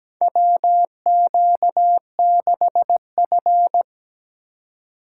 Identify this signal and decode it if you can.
Morse code, heard as a keyed tone, about 17 wpm, 700 Hz. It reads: WQ6F
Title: USA Call Signs - 4 Characters - 3x Character Spacing - 17wpm